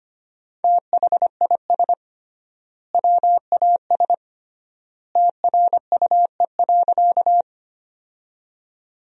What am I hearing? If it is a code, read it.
Morse code, heard as a keyed tone, about 25 wpm, 700 Hz. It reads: THIS WAS TRUE.
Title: 3x Word-Spacing - Sentences from Top 400 Words - 25wpm